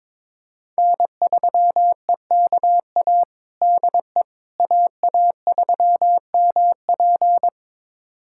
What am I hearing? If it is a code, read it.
Morse code, heard as a keyed tone, about 22 wpm, 700 Hz. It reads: N3EKA DE AA3MP
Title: USA Callsign DE Callsign - 5 Characters - 22wpm